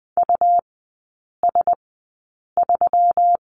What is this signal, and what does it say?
Morse code, heard as a keyed tone, about 20 wpm, spaced as for 8 wpm, 700 Hz. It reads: US3
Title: 121 Lesson 20e - Number 3 - Three Characters 20wpm